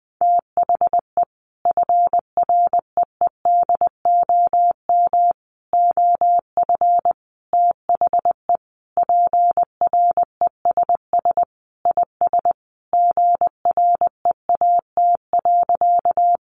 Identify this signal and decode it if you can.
Morse code, heard as a keyed tone, about 20 wpm, 700 Hz. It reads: THE FREEDOM OF THE PRESS IS GREAT.
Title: Sentences from Top 800 Words 20wpm